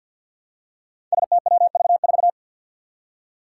Morse code, heard as a keyed tone, about 52 wpm, 700 Hz. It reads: STWV4
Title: Five Challenging Letter-Number Combinations 50wpm